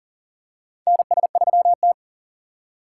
Morse code, heard as a keyed tone, about 40 wpm, 700 Hz. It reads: NS3T
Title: USA Call Signs - 4 Characters - 4x Character Spacing - 40wpm